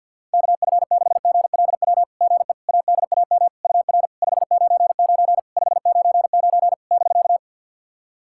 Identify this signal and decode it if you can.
Morse code, heard as a keyed tone, about 50 wpm, 700 Hz. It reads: KF6GLW GE ADAM UR 599 599 <BK>
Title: SOTA QSO Practice 50wpm